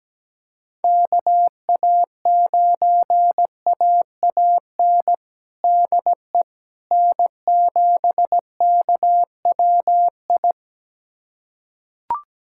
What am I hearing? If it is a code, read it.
Morse code, heard as a keyed tone, about 17 wpm, 700 Hz. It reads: KA9AAN DE N7KWI
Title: USA Callsign DE Callsign 17wpm